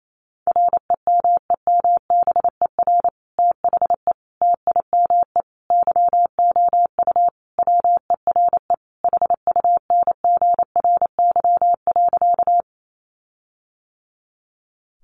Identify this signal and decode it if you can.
Morse code, heard as a keyed tone, about 28 wpm, 700 Hz. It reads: REMEMBER THE TIME YOU WERE HUNGRY.
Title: Sentences from Top 1400 Words 28wpm